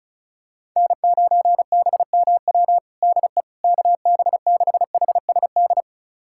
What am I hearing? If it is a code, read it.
Morse code, heard as a keyed tone, about 36 wpm, 700 Hz. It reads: N9BMW DE KB6HSD
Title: USA Callsign DE Callsign 35wpm